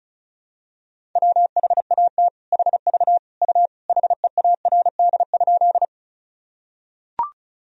Morse code, heard as a keyed tone, about 36 wpm, 700 Hz. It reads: WHAT HV U HEARD?
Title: Sentences from Top 400 Words and Abbreviations 35wpm